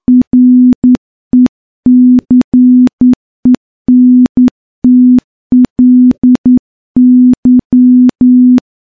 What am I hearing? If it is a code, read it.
Morse code, heard as a keyed tone, about 10 wpm, 250 Hz. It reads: RECENTLY
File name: morse-training-10WPM